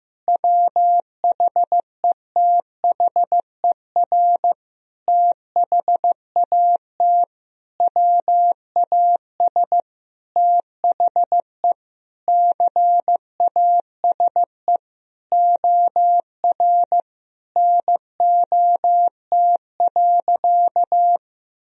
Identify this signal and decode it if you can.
Morse code, heard as a keyed tone, about 15 wpm, 700 Hz. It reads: WHETHER THAT WAS THE CASE OR NOT.
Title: Sentences from Top 1000 Words - Review Part 3 - 15wpm